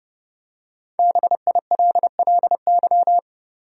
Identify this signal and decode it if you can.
Morse code, heard as a keyed tone, about 31 wpm, 700 Hz. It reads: BILLY
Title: USA Common Names - 30wpm